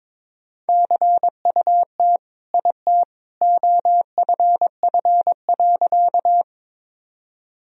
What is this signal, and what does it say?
Morse code, heard as a keyed tone, about 21 wpm, 700 Hz. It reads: CUT IT OFF.
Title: Sentences from Top 1000 Words - Review Part 1 - 22wpm